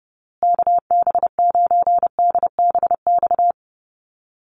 Morse code, heard as a keyed tone, about 30 wpm, 700 Hz. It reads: KB9DBX
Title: USA Call Signs Repeated 3x 30wpm